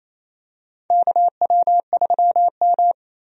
Morse code, heard as a keyed tone, about 28 wpm, 700 Hz. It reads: KW3M
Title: USA Call Signs - 4 Characters - 28wpm